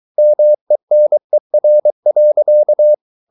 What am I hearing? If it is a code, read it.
Morse code, heard as a keyed tone, about 23 wpm, 600 Hz. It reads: MENER.
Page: Ferie 23 wpm